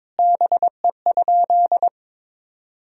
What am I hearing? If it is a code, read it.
Morse code, heard as a keyed tone, about 22 wpm, 700 Hz. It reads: BE?
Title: 3x Word-Spacing - Sentences from Top 100 Words - 22wpm